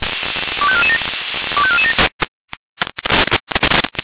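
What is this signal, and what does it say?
Five-tone selective call (ZVEI), heard as quick runs of five tones, tones of about 70 ms. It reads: 25447 25447
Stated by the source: ZVEI five-tone selective call received on an SDR.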